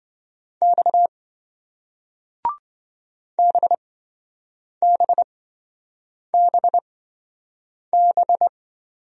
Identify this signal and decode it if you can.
Morse code, heard as a keyed tone, about 28 wpm, spaced as for 7 wpm, 700 Hz. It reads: X BBBB